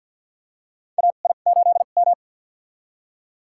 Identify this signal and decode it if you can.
Morse code, heard as a keyed tone, about 52 wpm, spaced as for 37 wpm, 700 Hz. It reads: AI8M
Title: USA Call Signs - 4 Characters - 2x Character Spacing - 50wpm